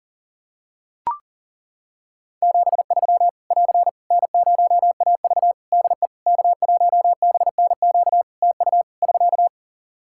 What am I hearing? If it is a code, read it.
Morse code, heard as a keyed tone, about 40 wpm, 700 Hz. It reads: 73 <AR> N0AV DE K1BNQ TU <SK>
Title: Practice QSO - Exchange 4 - Ending - 40wpm